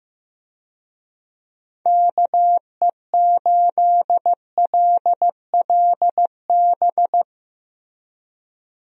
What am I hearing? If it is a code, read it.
Morse code, heard as a keyed tone, about 16 wpm, 700 Hz. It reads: KE8LLB
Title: USA Call Signs - 6 Characters - Repeated 2x - 15wpm